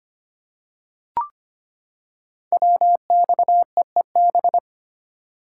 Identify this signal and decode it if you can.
Morse code, heard as a keyed tone, about 25 wpm, 690 Hz. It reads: WXEEB
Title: Code Groups - 5 Letters - 25wpm